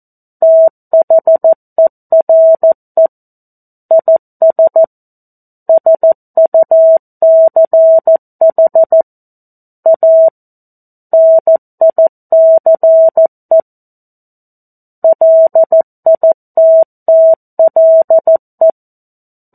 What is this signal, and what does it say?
Morse code, heard as a keyed tone, about 14 wpm, 650 Hz. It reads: THERE IS SUCH A NICE LITTLE
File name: 02-Chapter-02 14 WPM